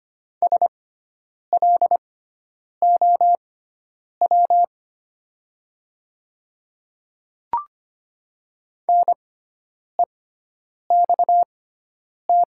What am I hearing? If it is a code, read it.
Morse code, heard as a keyed tone, about 25 wpm, spaced as for 9 wpm, 700 Hz. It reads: SLOW NEXT
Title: 6x Character Spacing - Top 500 Words - 25wpm